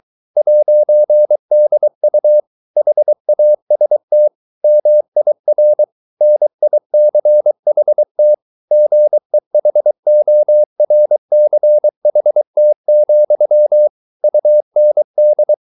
Morse code, heard as a keyed tone, about 23 wpm, 600 Hz. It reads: 'DU HAST MIR NICHT GEHORCHT, UND